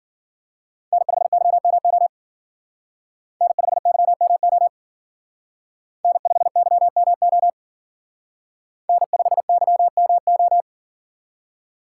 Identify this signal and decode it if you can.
Morse code, heard as a keyed tone, about 51 wpm, 700 Hz. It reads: N5YMO N5YMO N5YMO N5YMO